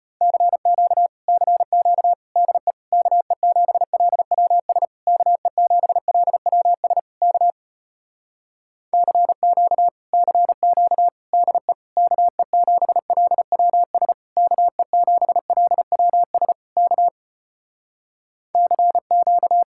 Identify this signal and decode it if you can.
Morse code, heard as a keyed tone, about 33 wpm, 700 Hz. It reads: CQ CQ DE KE7LWS KE7LWS K CQ CQ DE KE7LWS KE7LWS K CQ